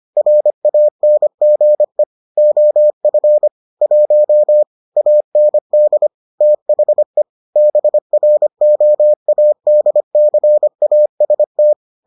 Morse code, heard as a keyed tone, about 25 wpm, 600 Hz. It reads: RANGE OF 1 AND THE BROADCAST